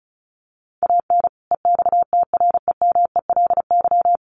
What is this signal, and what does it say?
Morse code, heard as a keyed tone, about 34 wpm, 700 Hz. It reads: AN EXTREMELY
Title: Sentences from Top 1200 Words 35wpm